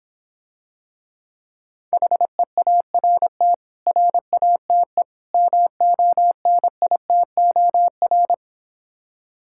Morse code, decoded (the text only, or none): HEART RATE MONITOR